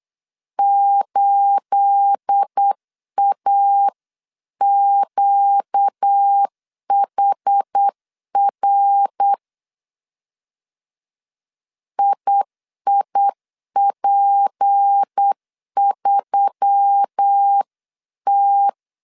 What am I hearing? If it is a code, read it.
Morse code, heard as a keyed tone, about 8 wpm, 800 Hz. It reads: 8AQHR IIP3T
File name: CW / DET / 1 9 1500C S A 8wpm 5637